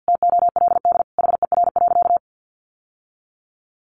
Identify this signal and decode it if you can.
Morse code, heard as a keyed tone, about 50 wpm, 700 Hz. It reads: TOLD HER.
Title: Sentences from Top 1200 Words 50wpm